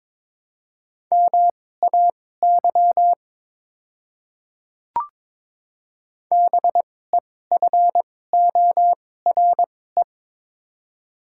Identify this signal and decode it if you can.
Morse code, heard as a keyed tone, about 22 wpm, spaced as for 16 wpm, 700 Hz. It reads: MAY BEFORE
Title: 2x Character Spacing - Top 300 Words - 22wpm